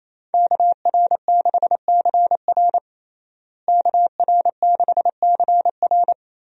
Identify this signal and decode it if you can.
Morse code, heard as a keyed tone, about 28 wpm, 700 Hz. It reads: KR6CR KR6CR
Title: USA Call Signs - 5 Characters - Repeated 3x - 28wpm